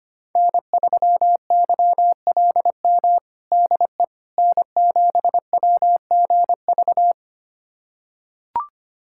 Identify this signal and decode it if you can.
Morse code, heard as a keyed tone, about 25 wpm, 700 Hz. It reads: N3YLM DE N7WGV
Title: USA Callsign DE Callsign - 5 Characters - 25wpm